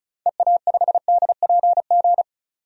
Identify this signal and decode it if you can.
Morse code, heard as a keyed tone, about 36 wpm, 700 Hz. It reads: EA5DPG
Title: DXCC-100 Call Signs - 6 Characters - 35wpm